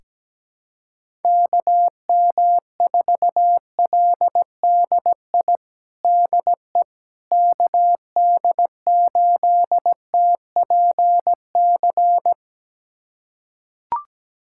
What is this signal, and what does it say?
Morse code, heard as a keyed tone, about 17 wpm, 700 Hz. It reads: KM4LDI DE KD8TPC
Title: USA Callsign DE Callsign - 6 Characters - 17wpm